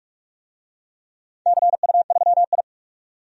Morse code, heard as a keyed tone, about 46 wpm, 700 Hz. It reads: CU3I
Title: DXCC-100 Call Signs - 4 Characters - 45wpm